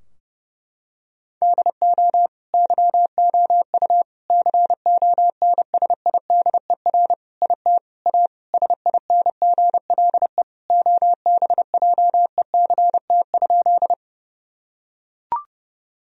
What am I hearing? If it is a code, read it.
Morse code, heard as a keyed tone, about 30 wpm, 700 Hz. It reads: DO YOU CONSIDER IT A SINGLE OBJECT?
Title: Sentences from Top 1000 Words - Review Part 4 - 30wpm